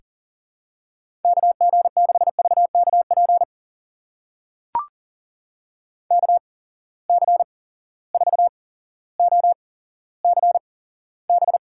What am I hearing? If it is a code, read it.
Morse code, heard as a keyed tone, about 40 wpm, spaced as for 17 wpm, 700 Hz. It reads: KG6VKP KC4OCB
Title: USA Call Signs - 6 Characters - 8x Character Spacing - 40wpm